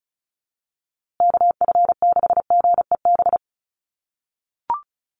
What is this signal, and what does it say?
Morse code, heard as a keyed tone, about 34 wpm, 700 Hz. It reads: KF6GEB
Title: USA Call Signs Repeated 3x 35wpm